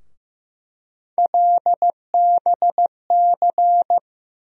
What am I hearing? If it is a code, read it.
Morse code, heard as a keyed tone, about 15 wpm, 700 Hz. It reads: LBC
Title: USA Call Sign Postfixes - 3 Letters 15wpm